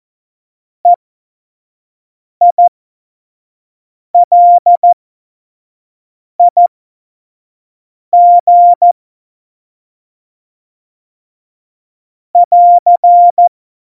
Morse code, heard as a keyed tone, about 14 wpm, spaced as for 5 wpm, 700 Hz. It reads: EILIG <AR>
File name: Morsekurs / Witze / Witz 10. Oktober 17, Tempo 25